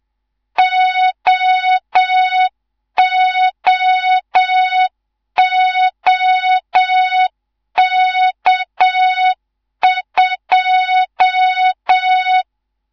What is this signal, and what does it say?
Morse code, heard as a keyed tone, about 7 wpm, 3700 Hz, with noise. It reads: OOOK2
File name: Morze-HA2MN / hang / 19 GYAKORLAT